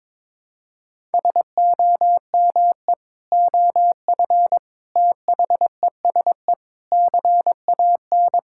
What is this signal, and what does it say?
Morse code, heard as a keyed tone, about 22 wpm, 700 Hz. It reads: SOME OF THESE CAN
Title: Sentences from Top 1000 Words - Review Part 6 - 22wpm